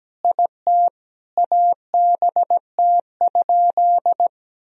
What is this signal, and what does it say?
Morse code, heard as a keyed tone, about 17 wpm, 700 Hz. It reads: IT ABT?
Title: Sentences from Top 100 Words and Abbreviations 17wpm